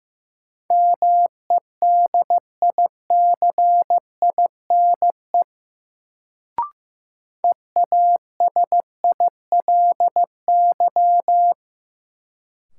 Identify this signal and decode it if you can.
Morse code, heard as a keyed tone, about 15 wpm, 700 Hz. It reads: MEDICINE EASILY
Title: Sentences from Top 900 Words - New Word Drill - 15wpm